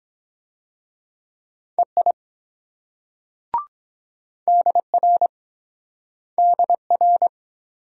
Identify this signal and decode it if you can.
Morse code, heard as a keyed tone, about 23 wpm, 700 Hz. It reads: EI DR DR